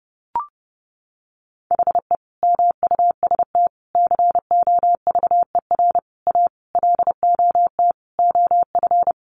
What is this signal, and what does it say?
Morse code, heard as a keyed tone, about 30 wpm, 700 Hz. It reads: HE MUST COVER A LOT OF
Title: Sentences from Top 500 Words 30wpm